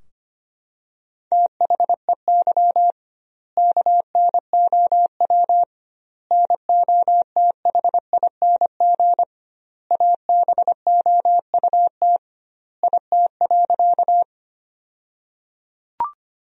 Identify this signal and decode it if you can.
Morse code, heard as a keyed tone, about 25 wpm, 700 Hz. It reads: THEY KNOW NOTHING ABOUT IT.
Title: 2x Word-Spacing - Sentences from Top 500 Words - 25wpm